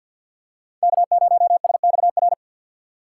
Morse code, heard as a keyed tone, about 50 wpm, 700 Hz. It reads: K0SXR
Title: USA Call Signs - 5 Characters - 2x Character Spacing - 50wpm